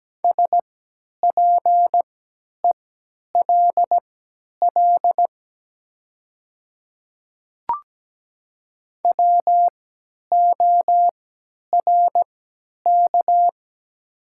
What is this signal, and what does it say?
Morse code, heard as a keyed tone, about 17 wpm, spaced as for 10 wpm, 700 Hz. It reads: SPELL WORK
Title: 3x Character Spacing - Top 200 Words - 17wpm